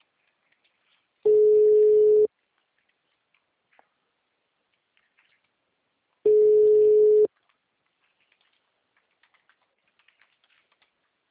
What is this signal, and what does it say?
Ringback tone (Europe, 425 Hz).
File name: out- -2017- - - .47